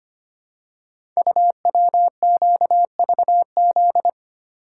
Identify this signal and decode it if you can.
Morse code, heard as a keyed tone, about 25 wpm, 700 Hz. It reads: UWQVZ
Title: Code Groups - 5 Letters - 25wpm